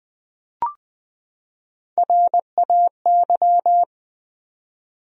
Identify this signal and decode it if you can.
Morse code, heard as a keyed tone, about 20 wpm, 700 Hz. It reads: RAY
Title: USA Common Names - Max 4 Letters - 20wpm